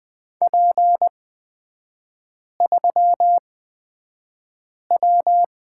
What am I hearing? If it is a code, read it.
Morse code, heard as a keyed tone, about 20 wpm, spaced as for 5 wpm, 700 Hz. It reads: P3W
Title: 224 Lesson 36e - BT - Three Characters 20wpm